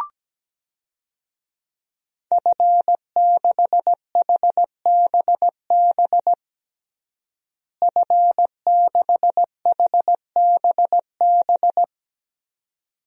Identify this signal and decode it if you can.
Morse code, heard as a keyed tone, about 17 wpm, 700 Hz. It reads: F6HBB F6HBB
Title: DXCC-100 Call Signs - 5 Characters - Repeated 2x - 17wpm